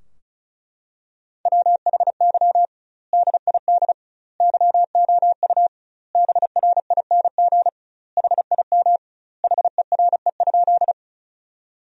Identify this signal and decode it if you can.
Morse code, heard as a keyed tone, about 36 wpm, 700 Hz. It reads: WHY DID YOU BRING HIM HERE?
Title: 2x Word-Spacing - Sentences from Top 500 Words - 35wpm